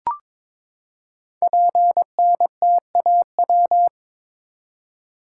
Morse code, heard as a keyed tone, about 22 wpm, 700 Hz. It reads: PNTAW
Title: Code Groups - 5 Letters - 22wpm